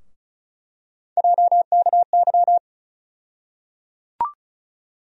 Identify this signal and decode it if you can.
Morse code, heard as a keyed tone, about 34 wpm, 700 Hz. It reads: JKY
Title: USA Call Sign Postfixes - 3 Letters 35wpm